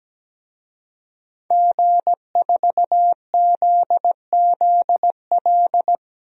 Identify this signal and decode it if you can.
Morse code, heard as a keyed tone, about 17 wpm, 700 Hz. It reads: G4ZZL
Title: DXCC-100 Call Signs - 5 Characters - Repeated 3x - 17wpm